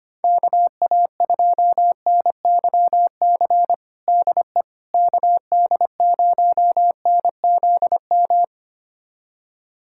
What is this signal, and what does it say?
Morse code, heard as a keyed tone, about 25 wpm, 700 Hz. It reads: KA2NYC DE KD0NZM
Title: USA Callsign DE Callsign - 6 Characters - 25wpm